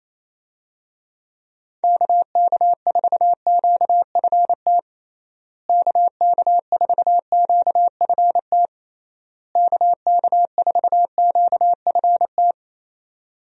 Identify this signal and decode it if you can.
Morse code, heard as a keyed tone, about 27 wpm, 700 Hz. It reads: KK4QFT KK4QFT KK4QFT
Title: USA Call Signs - 6 Characters - Repeated 3x - 28wpm